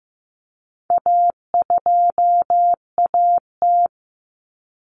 Morse code, heard as a keyed tone, about 15 wpm, 700 Hz. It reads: A2AT
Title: Common Cut Numbers - 4-Digit 15wpm